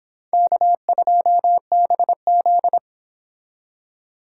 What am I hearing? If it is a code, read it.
Morse code, heard as a keyed tone, about 29 wpm, 700 Hz. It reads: K2BZ